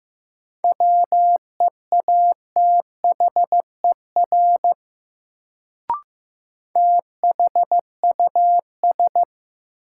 Morse code, heard as a keyed tone, about 15 wpm, 700 Hz. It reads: WEATHER THUS
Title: Sentences from Top 800 Words - New Word Drill - 15wpm